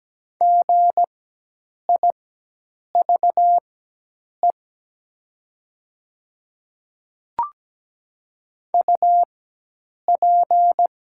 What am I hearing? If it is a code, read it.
Morse code, heard as a keyed tone, about 17 wpm, spaced as for 8 wpm, 700 Hz. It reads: GIVE UP
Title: 4x Character Spacing - Top 200 Words - 17wpm